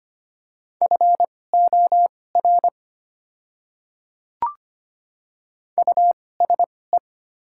Morse code, heard as a keyed tone, about 25 wpm, spaced as for 18 wpm, 690 Hz. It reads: FOR USE
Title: 2x Character Spacing - Top 200 Words - 25wpm